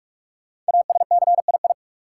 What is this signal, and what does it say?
Morse code, heard as a keyed tone, about 42 wpm, 700 Hz. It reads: ASCII
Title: Sets of 5 Words - New Word Drill - 45wpm